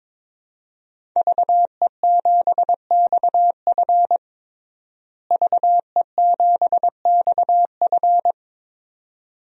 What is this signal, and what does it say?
Morse code, heard as a keyed tone, about 21 wpm, 700 Hz. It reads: VE7XF VE7XF
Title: DXCC-100 Call Signs - 5 Characters - Repeated 2x - 22wpm